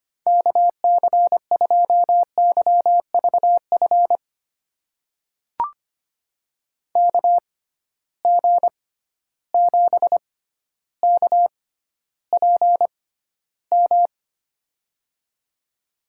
Morse code, heard as a keyed tone, about 25 wpm, spaced as for 13 wpm, 700 Hz. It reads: KC2YVF KG7KPM
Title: USA Call Signs - 6 Characters - 6x Character Spacing - 25wpm